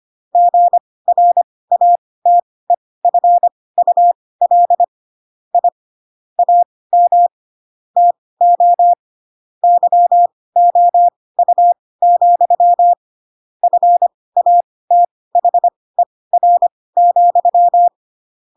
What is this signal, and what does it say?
Morse code, heard as a keyed tone, about 25 wpm, spaced as for 18 wpm, 700 Hz. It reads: GRATEFUL I AM TO YOU, FATHER,